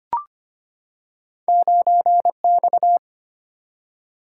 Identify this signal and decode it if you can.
Morse code, heard as a keyed tone, about 25 wpm, 700 Hz. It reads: 9X
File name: Two Letter-Number Combinations - 25wpm